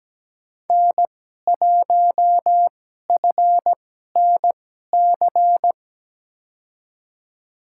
Morse code, heard as a keyed tone, about 17 wpm, spaced as for 12 wpm, 700 Hz. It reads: N1FNC
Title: USA Call Signs - 5 Characters - 2x Character Spacing - 17wpm